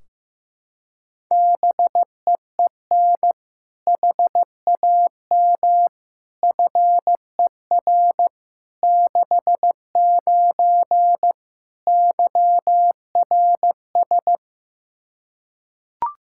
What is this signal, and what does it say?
Morse code, heard as a keyed tone, about 15 wpm, 700 Hz. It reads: BEEN HAM FER 69 YRS
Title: Practice QSO - Exchange 3 - More Info - 15wpm